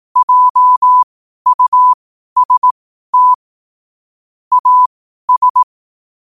Morse code, heard as a keyed tone, about 18 wpm, spaced as for 12 wpm, 1000 Hz. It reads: JUST AS
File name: qotdfiles / 12wpm